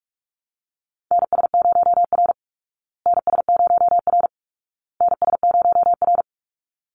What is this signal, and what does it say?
Morse code, heard as a keyed tone, about 44 wpm, 700 Hz. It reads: NS0R NS0R NS0R
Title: Call Signs Repeated 2x - 45wpm